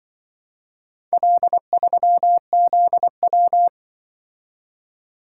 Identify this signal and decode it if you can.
Morse code, heard as a keyed tone, about 23 wpm, 700 Hz. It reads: L3ZW
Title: Speed-Racing - Four Letter-Number Combinations - 20wpm